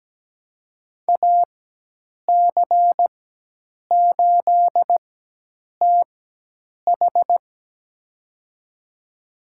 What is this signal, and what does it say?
Morse code, heard as a keyed tone, about 17 wpm, spaced as for 8 wpm, 700 Hz. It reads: AC8TH
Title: USA Call Signs - 4x Character Spacing - 17wpm